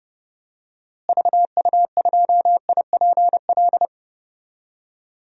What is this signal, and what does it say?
Morse code, heard as a keyed tone, about 32 wpm, 700 Hz. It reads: VU2IPL
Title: DXCC-100 Call Signs - 6 Characters - Repeated 3x - 30wpm